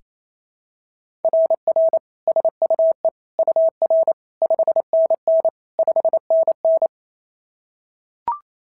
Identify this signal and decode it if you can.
Morse code, heard as a keyed tone, about 27 wpm, 650 Hz. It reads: RR SUE UR 5NN 5NN
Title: SOTA QSO Practice 28wpm